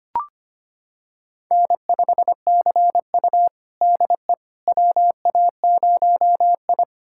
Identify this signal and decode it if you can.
Morse code, heard as a keyed tone, about 25 wpm, 700 Hz. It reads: N5CU DE WA0I
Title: USA Callsign DE Callsign - 4 Characters - 25wpm